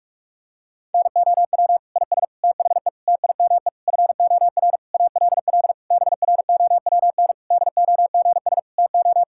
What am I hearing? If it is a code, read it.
Morse code, heard as a keyed tone, about 46 wpm, 700 Hz. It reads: NOW IS THE TIME FOR ALL BROWN DOGS TO
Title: Pangram Challenge v2 - 45wpm